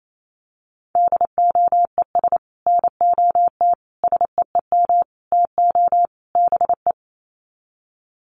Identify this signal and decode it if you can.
Morse code, heard as a keyed tone, about 28 wpm, 700 Hz. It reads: DOES NOT SEEM TO BE
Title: Sets of 5 Words 28wpm